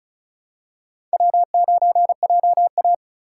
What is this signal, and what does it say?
Morse code, heard as a keyed tone, about 34 wpm, 700 Hz. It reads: W9JA
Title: USA Call Signs - 4 Characters - 3x Character Spacing - 35wpm